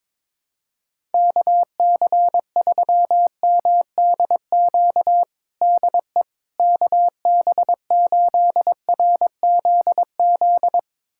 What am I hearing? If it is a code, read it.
Morse code, heard as a keyed tone, about 22 wpm, 700 Hz. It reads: KC3MDQ DE KB8RZZ
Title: USA Callsign DE Callsign 22wpm